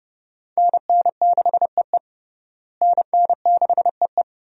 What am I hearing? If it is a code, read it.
Morse code, heard as a keyed tone, about 31 wpm, 700 Hz. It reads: NN6EE NN6EE
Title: USA Call Signs - 5 Characters - Repeated 2x - 30wpm